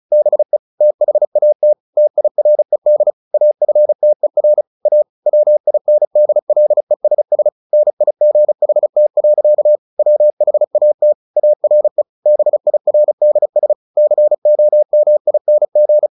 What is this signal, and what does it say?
Morse code, heard as a keyed tone, about 34 wpm, 600 Hz. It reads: BE THAT TIRED AFTER A WINDLESS NIGHT. WHAT ARE BIRDS COMING
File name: the old man and the sea / the old man and the sea 1 35wpm